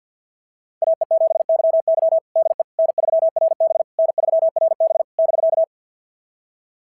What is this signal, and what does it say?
Morse code, heard as a keyed tone, about 46 wpm, 660 Hz. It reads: AE7YY DE N3RB N3RB <BK>
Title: SOTA QSO Practice 50wpm